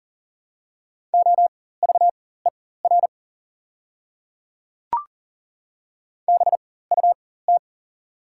Morse code, heard as a keyed tone, about 42 wpm, spaced as for 19 wpm, 700 Hz. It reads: OVER BUT
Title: 4x Character Spacing - Top 100 Words - 40wpm